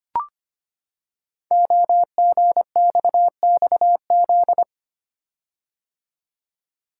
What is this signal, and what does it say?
Morse code, heard as a keyed tone, about 25 wpm, 700 Hz. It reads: OGXXZ
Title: Code Groups - 5 Letter:Numbers - 25wpm